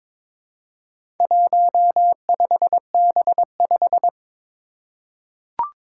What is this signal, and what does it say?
Morse code, heard as a keyed tone, about 22 wpm, 700 Hz. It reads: 15B5
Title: Four Challenging Letter-Number Combinations 22wpm